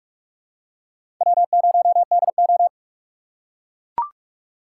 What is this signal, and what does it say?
Morse code, heard as a keyed tone, about 42 wpm, 700 Hz. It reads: W0DO
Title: USA Call Signs - 4 Characters - 4x Character Spacing - 45wpm